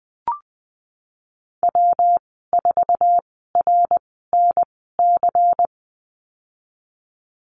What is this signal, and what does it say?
Morse code, heard as a keyed tone, about 20 wpm, spaced as for 15 wpm, 700 Hz. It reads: W4RNC
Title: 122 Lesson 20g - Number 3 - Call Signs 20wpm